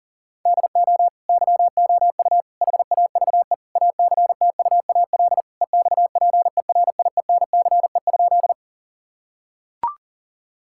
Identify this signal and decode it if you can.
Morse code, heard as a keyed tone, about 37 wpm, 700 Hz. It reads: DO YOU HAVE ACTUAL EXPERIENCE?
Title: Sentences from Top 1000 Words - Review Part 5 - 40wpm